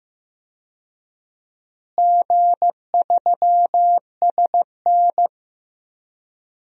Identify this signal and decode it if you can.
Morse code, heard as a keyed tone, about 15 wpm, 700 Hz. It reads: G3SN
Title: DXCC-100 Call Signs - 4 Characters - Repeated 2x - 15wpm